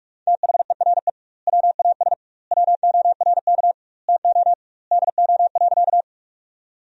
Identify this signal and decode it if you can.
Morse code, heard as a keyed tone, about 42 wpm, 700 Hz. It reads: THERE WAS WORK TO DO.
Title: 2x Word-Spacing - Sentences from Top 200 Words - 45wpm